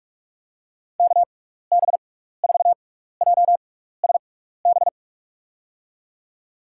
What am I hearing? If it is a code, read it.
Morse code, heard as a keyed tone, about 47 wpm, spaced as for 16 wpm, 700 Hz. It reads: KB4JSB
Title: USA Call Signs - 6 Characters - 6x Character Spacing - 45wpm